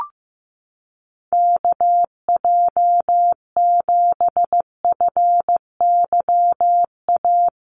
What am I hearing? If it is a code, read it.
Morse code, heard as a keyed tone, about 15 wpm, 700 Hz. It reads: KJ7FYA
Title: USA Call Signs Repeated 2x 15wpm